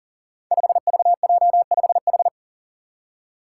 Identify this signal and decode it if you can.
Morse code, heard as a keyed tone, about 40 wpm, 700 Hz. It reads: 5VJ5H
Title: Five Challenging Letter-Number Combinations 40wpm